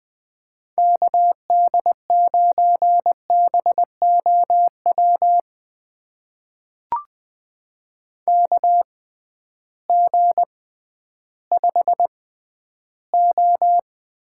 Morse code, heard as a keyed tone, about 20 wpm, spaced as for 12 wpm, 700 Hz. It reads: KD9BOW KG5O
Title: USA Call Signs - 6 Characters - 6x Character Spacing - 20wpm